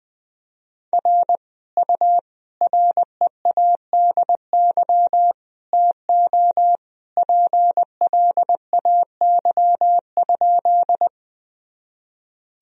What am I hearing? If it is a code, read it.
Morse code, heard as a keyed tone, about 20 wpm, 700 Hz. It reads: R U READY TO PLAY?
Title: Sentences from Top 400 Words and Abbreviations 20wpm